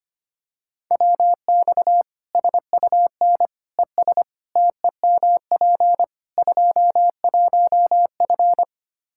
Morse code, heard as a keyed tone, about 25 wpm, 700 Hz. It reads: WX SUN ES TEMP 21F
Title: Practice QSO - Exchange 2 - Info - 25wpm